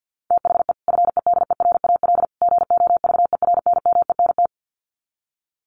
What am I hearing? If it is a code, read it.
Morse code, heard as a keyed tone, about 50 wpm, 700 Hz. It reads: THE FEDERAL GOVERNMENT
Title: Sets of 3 Words 50wpm